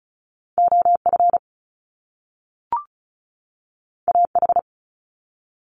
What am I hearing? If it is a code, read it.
Morse code, heard as a keyed tone, about 35 wpm, 700 Hz. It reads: OF AH
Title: Top Two Letter Combinations in Words 35wpm